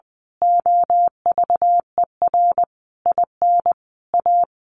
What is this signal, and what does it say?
Morse code, heard as a keyed tone, about 20 wpm, 700 Hz. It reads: OVER IN A
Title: A Sound Of Thunder 20wpm